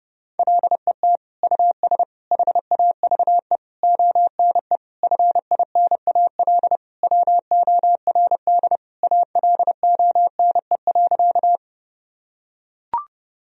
Morse code, heard as a keyed tone, about 30 wpm, 700 Hz. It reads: LET US HAVE ONE FINAL WORD ALONE.
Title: Sentences from Top 1000 Words - Review Part 2 - 30wpm